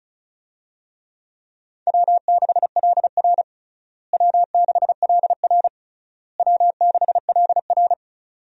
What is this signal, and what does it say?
Morse code, heard as a keyed tone, about 34 wpm, 700 Hz. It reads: W6LR W6LR W6LR
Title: USA Call Signs - 4 Characters - Repeated 3x - 35wpm